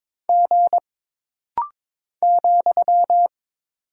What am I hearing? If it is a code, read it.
Morse code, heard as a keyed tone, about 22 wpm, 700 Hz. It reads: G ,
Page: This rapid-fire timing is geared to push your ICR (Instant Character Recognition) skills.